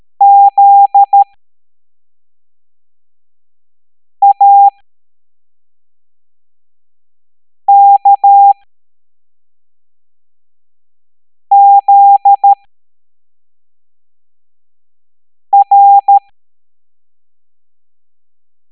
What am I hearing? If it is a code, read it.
Morse code, heard as a keyed tone, about 13 wpm, spaced as for 3 wpm, 800 Hz. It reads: ZAKZR